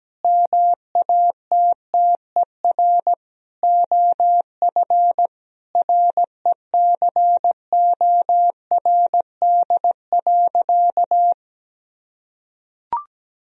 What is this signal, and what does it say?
Morse code, heard as a keyed tone, about 17 wpm, 700 Hz. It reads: MATTER OF RECORD.
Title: Sentences from Top 1000 Words - Review Part 3 - 17wpm